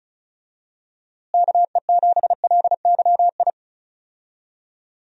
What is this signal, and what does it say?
Morse code, heard as a keyed tone, about 36 wpm, 700 Hz. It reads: KE7LYI
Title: USA Call Signs - 6 Characters - Repeated 2x - 35wpm